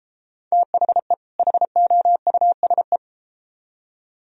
Morse code, heard as a keyed tone, about 33 wpm, 700 Hz. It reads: THE HOUSE